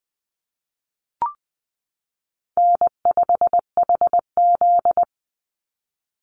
Morse code, heard as a keyed tone, about 20 wpm, 700 Hz. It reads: N5HZ
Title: Easy Call Signs - 20wpm